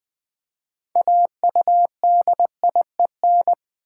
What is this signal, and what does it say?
Morse code, heard as a keyed tone, about 20 wpm, 700 Hz. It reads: AUDIEN
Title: Build A Word - 3 Letters At A Time - 8-Letter Words - 20wpm